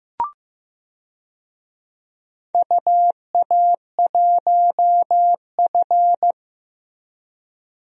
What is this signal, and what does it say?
Morse code, heard as a keyed tone, about 15 wpm, 700 Hz. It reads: UA1F
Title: DXCC-100 Call Signs - 4 Characters - Repeated 3x - 15wpm